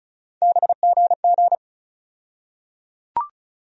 Morse code, heard as a keyed tone, about 36 wpm, 700 Hz. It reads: BGG
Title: USA Call Sign Postfixes - 3 Letters 35wpm